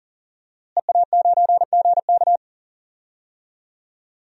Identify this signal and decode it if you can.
Morse code, heard as a keyed tone, about 40 wpm, 700 Hz. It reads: EA9GK
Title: DXCC-100 Call Signs - 5 Characters - Repeated 2x - 40wpm